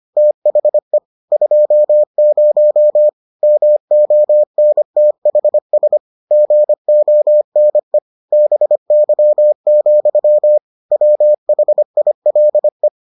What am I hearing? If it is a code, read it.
Morse code, heard as a keyed tone, about 25 wpm, 600 Hz. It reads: THE 20 MONTHS GONE BY, WHILE